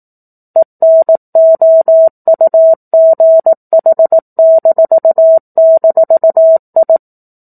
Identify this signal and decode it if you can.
Morse code, heard as a keyed tone, about 18 wpm, 650 Hz. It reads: ENOUGH--I
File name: 04-Chapter-04 18 WPM